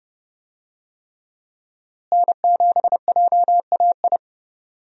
Morse code, heard as a keyed tone, about 31 wpm, 700 Hz. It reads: N7JAI
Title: USA Call Signs - 5 Characters - Repeated 3x - 30wpm